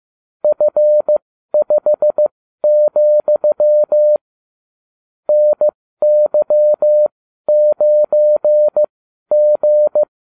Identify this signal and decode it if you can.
Morse code, heard as a keyed tone, about 15 wpm, spaced as for 12 wpm, 600 Hz. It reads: F5, NY9G